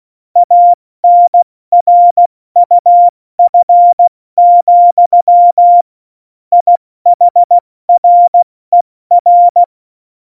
Morse code, heard as a keyed tone, about 16 wpm, spaced as for 14 wpm, 700 Hz. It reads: ANRUF, IHRER
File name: Morsekurs / Witze / Witz 01. Oktober 2019, 70 F80